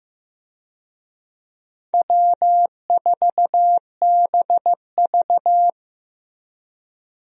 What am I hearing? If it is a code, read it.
Morse code, heard as a keyed tone, about 15 wpm, 700 Hz. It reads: W4BV
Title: USA Call Signs - 4 Characters - Repeated 2x - 15wpm